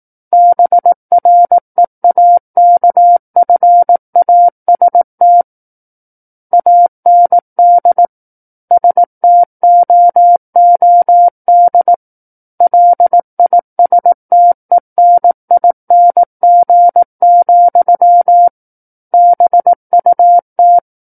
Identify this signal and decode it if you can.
Morse code, heard as a keyed tone, about 18 wpm, 700 Hz. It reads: BREAKFAST AND STOOD LISTENING, BUT
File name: 18 WPM